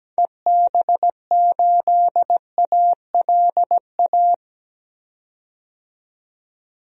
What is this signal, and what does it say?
Morse code, heard as a keyed tone, about 17 wpm, 700 Hz. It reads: EB8ALA
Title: DXCC-100 Call Signs - 6 Characters - 17wpm